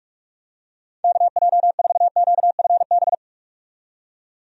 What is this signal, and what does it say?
Morse code, heard as a keyed tone, about 46 wpm, 700 Hz. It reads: KJ4QFB
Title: USA Call Signs - 6 Characters - 45wpm